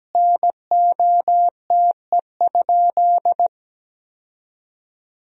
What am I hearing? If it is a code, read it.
Morse code, heard as a keyed tone, about 17 wpm, 700 Hz. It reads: NOTE?
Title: 4x Word-Spacing - Sentences from Top 500 Words - 17wpm